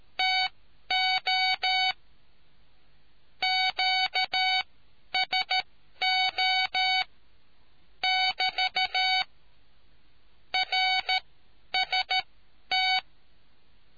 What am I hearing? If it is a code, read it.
Morse code, heard as a keyed tone, about 13 wpm, spaced as for 10 wpm, 750 Hz. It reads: TO QSO <BT> RST